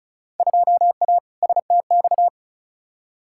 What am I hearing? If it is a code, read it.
Morse code, heard as a keyed tone, about 36 wpm, 700 Hz. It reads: 2A STX
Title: Field Day Exchanges - 35wpm